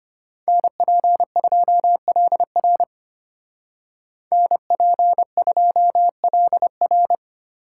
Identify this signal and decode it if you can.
Morse code, heard as a keyed tone, about 33 wpm, 700 Hz. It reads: NP2LR NP2LR